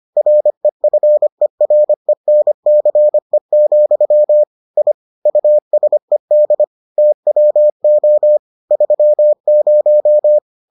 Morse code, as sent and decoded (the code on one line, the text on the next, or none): .-. ..-. .-. -. -.-. --..-- / .. / ..- ... -.. / - .-- --- / ...-- -----
REFERENCE, I USED TWO 30